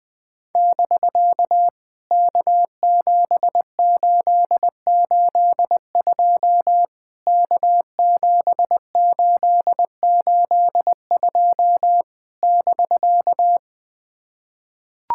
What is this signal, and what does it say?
Morse code, heard as a keyed tone, about 19 wpm, 700 Hz. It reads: <BK> K7882 K7882 <BK>
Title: POTA QSO Practice - 20wpm